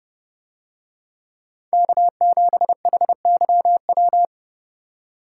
Morse code, heard as a keyed tone, about 29 wpm, 700 Hz. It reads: K7HYW